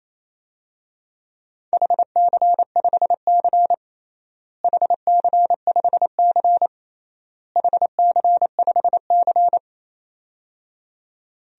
Morse code, heard as a keyed tone, about 28 wpm, 700 Hz. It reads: HC5C HC5C HC5C
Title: DXCC-100 Call Signs - 4 Characters - Repeated 3x - 28wpm